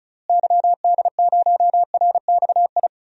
Morse code, heard as a keyed tone, about 36 wpm, 700 Hz. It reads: YD0RXI
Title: DXCC-100 Call Signs - 6 Characters - Repeated 2x - 35wpm